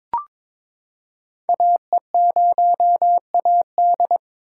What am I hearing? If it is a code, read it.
Morse code, heard as a keyed tone, about 22 wpm, 700 Hz. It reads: AE0AD
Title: Code Groups - 5 Letter:Numbers - 22wpm